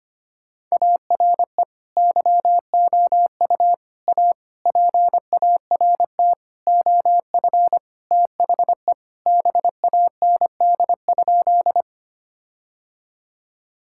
Morse code, heard as a keyed tone, about 25 wpm, 700 Hz. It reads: ARE YOU A PART OF THE BAND?
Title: Sentences from Top 1600 Words 25wpm